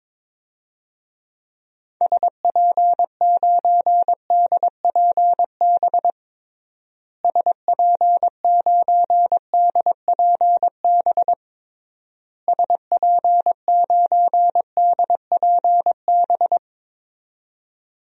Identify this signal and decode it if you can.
Morse code, heard as a keyed tone, about 21 wpm, 700 Hz. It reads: SP9DPB SP9DPB SP9DPB
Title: DXCC-100 Call Signs - 6 Characters - Repeated 3x - 22wpm